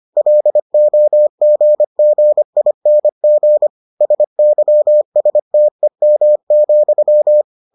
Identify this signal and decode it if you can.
Morse code, heard as a keyed tone, about 25 wpm, 600 Hz. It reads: LOGGING SYSTEM,